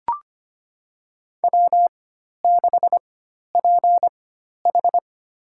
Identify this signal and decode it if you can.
Morse code, heard as a keyed tone, about 25 wpm, spaced as for 12 wpm, 700 Hz. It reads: W6PH
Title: USA Call Signs - 4 Characters - 4x Character Spacing - 25wpm